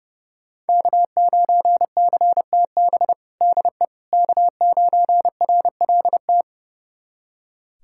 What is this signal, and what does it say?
Morse code, heard as a keyed tone, about 30 wpm, 700 Hz. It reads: K9CTB DE K9RLT
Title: USA Callsign DE Callsign - 5 Characters - 30wpm